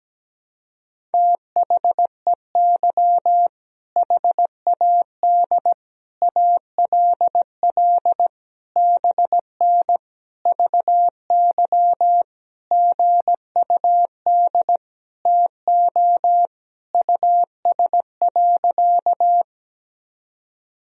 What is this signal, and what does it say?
Morse code, heard as a keyed tone, about 17 wpm, 700 Hz. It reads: THEY HAD ALL BN VY GUD TO US.
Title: Sentences from Top 200 Words and Abbreviations 17wpm